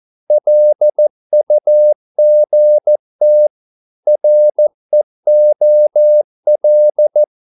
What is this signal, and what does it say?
Morse code, heard as a keyed tone, about 14 wpm, 600 Hz. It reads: LUGT REOL
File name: Korte ord DK 14wpm